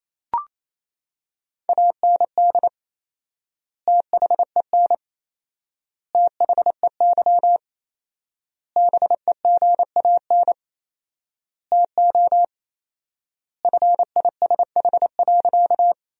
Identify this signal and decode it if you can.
Morse code, heard as a keyed tone, about 27 wpm, 700 Hz. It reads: AND THEN THEY BEGAN TO FISH.
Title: 4x Word-Spacing - Sentences from Top 300 Words - 28wpm